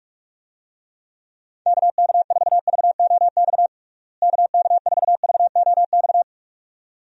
Morse code, heard as a keyed tone, about 46 wpm, 700 Hz. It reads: KK4VOX KK4VOX
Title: USA Call Signs - 6 Characters - Repeated 2x - 45wpm